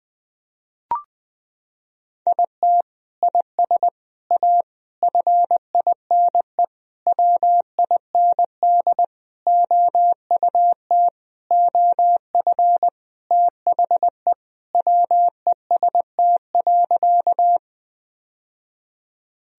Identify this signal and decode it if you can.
Morse code, heard as a keyed tone, about 20 wpm, 700 Hz. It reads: IT IS A FINE WIND OUT OF THE WEST.
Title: Sentences from Top 1000 Words - Review Part 2 - 20wpm